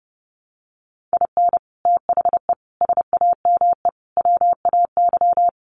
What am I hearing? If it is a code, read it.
Morse code, heard as a keyed tone, about 30 wpm, 700 Hz. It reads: IN THE SAME WAY
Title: Sets of 4 Words 30wpm